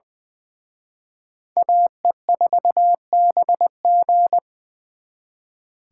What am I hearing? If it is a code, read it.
Morse code, heard as a keyed tone, about 24 wpm, 700 Hz. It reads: AE4BG
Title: Speed Racing - USA 2x2 Call Signs 20wpm